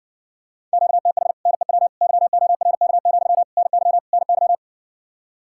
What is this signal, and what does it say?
Morse code, heard as a keyed tone, about 70 wpm, 700 Hz. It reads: QTH NEW YORK, NY NY